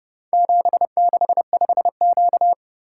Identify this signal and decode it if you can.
Morse code, heard as a keyed tone, about 30 wpm, 700 Hz. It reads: 765Q